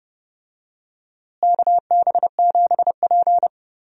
Morse code, heard as a keyed tone, about 30 wpm, 700 Hz. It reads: KB7P
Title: USA Call Signs - 4 Characters - Repeated 3x - 30wpm